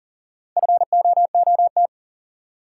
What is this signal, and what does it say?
Morse code, heard as a keyed tone, about 37 wpm, 700 Hz. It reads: FOOT
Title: Sentences from Top 500 Words - New Word Drill - 40wpm